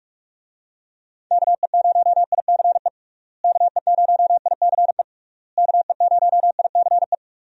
Morse code, heard as a keyed tone, about 46 wpm, 700 Hz. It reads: KE0ICE KE0ICE KE0ICE
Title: USA Call Signs - 6 Characters - Repeated 3x - 45wpm